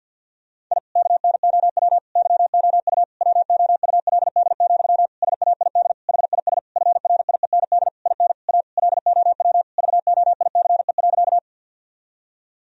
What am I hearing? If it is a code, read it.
Morse code, heard as a keyed tone, about 50 wpm, 700 Hz. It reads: I KNOW YOU WOULD, SAID HIS FRIEND IN A LOW VOICE.
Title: Sentences from Top 1000 Words - Review Part 1 - 50wpm